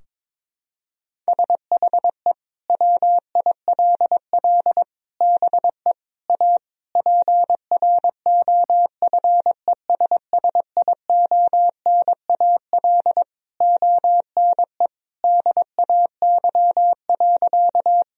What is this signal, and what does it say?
Morse code, heard as a keyed tone, about 22 wpm, 700 Hz. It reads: SHE WILL BE A PROFESSIONAL ONE DAY.
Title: Sentences from Top 1000 Words - Review Part 4 - 22wpm